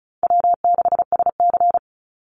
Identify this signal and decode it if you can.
Morse code, heard as a keyed tone, about 34 wpm, 700 Hz. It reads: W6SC
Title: Call Signs Repeated 2x - 35wpm